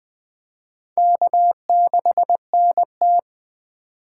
Morse code, heard as a keyed tone, about 20 wpm, 700 Hz. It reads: K6NT
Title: USA Call Signs - 4 Characters - 3x Character Spacing - 20wpm